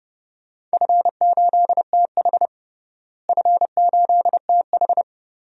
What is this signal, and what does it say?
Morse code, heard as a keyed tone, about 32 wpm, 700 Hz. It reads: F8TH F8TH
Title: DXCC-100 Call Signs - 4 Characters - Repeated 2x - 30wpm